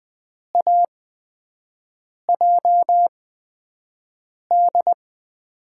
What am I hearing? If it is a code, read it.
Morse code, heard as a keyed tone, about 20 wpm, spaced as for 5 wpm, 700 Hz. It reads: AJD
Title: USA Call Signs - 6 Characters - 8x Character Spacing - 20wpm